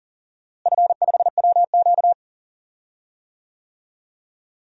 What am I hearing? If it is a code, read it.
Morse code, heard as a keyed tone, about 40 wpm, 700 Hz. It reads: F5WQ
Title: DXCC-100 Call Signs - 4 Characters - Repeated 2x - 40wpm